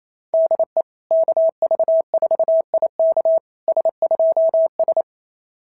Morse code, heard as a keyed tone, about 29 wpm, 660 Hz. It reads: DE KV4IK S2S
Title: SOTA QSO Practice 28wpm